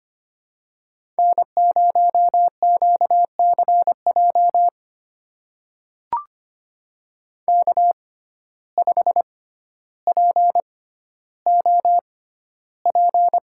Morse code, heard as a keyed tone, about 25 wpm, spaced as for 13 wpm, 700 Hz. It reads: N0QCJ K5POP
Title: USA Call Signs - 6x Character Spacing - 25wpm